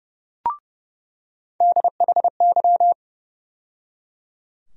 Morse code, heard as a keyed tone, about 31 wpm, 700 Hz. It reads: DHY
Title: USA Call Sign Postfixes - 3 Letters 30wpm